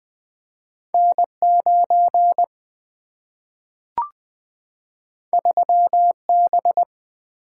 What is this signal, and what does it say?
Morse code, heard as a keyed tone, about 20 wpm, 700 Hz. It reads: N9 3B
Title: Single-Letter Single-Number 20wpm